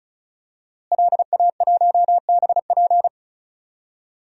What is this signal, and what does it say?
Morse code, heard as a keyed tone, about 34 wpm, 700 Hz. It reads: LA1BP
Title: DXCC-100 Call Signs - 5 Characters - Repeated 3x - 35wpm